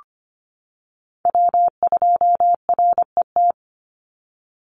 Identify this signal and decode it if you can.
Morse code, heard as a keyed tone, about 25 wpm, 700 Hz. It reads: W2RET
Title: USA Call Signs 25wpm